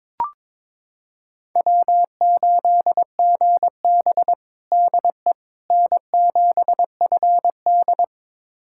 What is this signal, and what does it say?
Morse code, heard as a keyed tone, about 22 wpm, 700 Hz. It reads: W8GB DE N7FD
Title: USA Callsign DE Callsign - 4 Characters - 22wpm